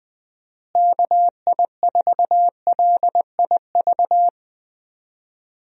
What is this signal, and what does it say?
Morse code, heard as a keyed tone, about 20 wpm, 700 Hz. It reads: KI4LIV
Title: USA Call Signs - 6 Characters - Repeated 3x - 20wpm